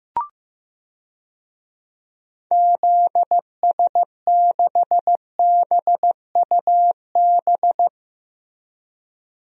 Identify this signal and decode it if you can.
Morse code, heard as a keyed tone, about 15 wpm, 700 Hz. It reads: ZS6BUB
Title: DXCC-100 Call Signs - 6 Characters - Repeated 2x - 15wpm